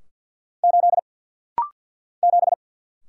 Morse code, heard as a keyed tone, about 50 wpm, 690 Hz. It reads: 8 7
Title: Single Letter-Number - Rapid-Fire 50wpm